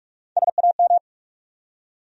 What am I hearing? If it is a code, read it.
Morse code, heard as a keyed tone, about 47 wpm, 700 Hz. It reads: SAM
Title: USA Common Names - Max 3 Letters - 45wpm